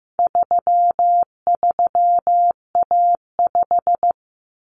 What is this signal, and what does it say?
Morse code, heard as a keyed tone, about 15 wpm, 700 Hz. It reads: 33A5
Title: Common Cut Numbers - 4-Digit 15wpm